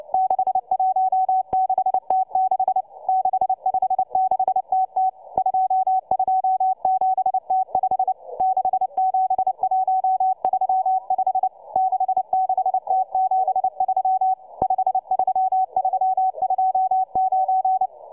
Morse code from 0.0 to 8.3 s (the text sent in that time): 616T6 656TT 227T5